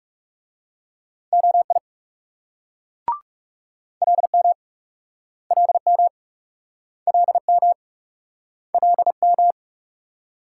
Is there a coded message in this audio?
Yes: Morse code from 1.3 to 9.5 s, heard as a keyed tone, about 40 wpm, 700 Hz. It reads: OI LM LM LM LM